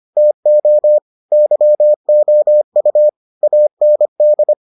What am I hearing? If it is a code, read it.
Morse code, heard as a keyed tone, about 25 wpm, 600 Hz. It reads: TO YOU AND